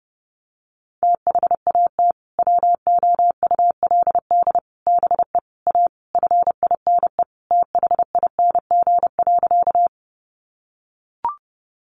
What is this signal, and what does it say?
Morse code, heard as a keyed tone, about 30 wpm, 700 Hz. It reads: THAT WOULD BE A FINE THING.
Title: Sentences from Top 500 Words 30wpm